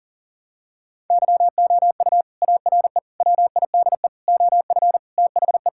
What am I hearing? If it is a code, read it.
Morse code, heard as a keyed tone, about 40 wpm, 700 Hz. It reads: YOU ARE WIDE OF THE
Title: Sentences from Top 1000 Words - Review Part 6 - 40wpm